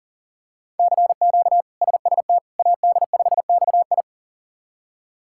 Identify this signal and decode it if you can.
Morse code, heard as a keyed tone, about 37 wpm, 700 Hz. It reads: CQ SST AD5XI
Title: K1USN SST QSO Practice - 40wpm